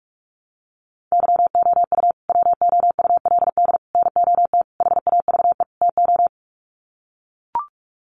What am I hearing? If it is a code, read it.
Morse code, heard as a keyed tone, about 44 wpm, 700 Hz. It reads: YOU WOULD NOT HAVE TO
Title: Sets of 5 Words Encore 45wpm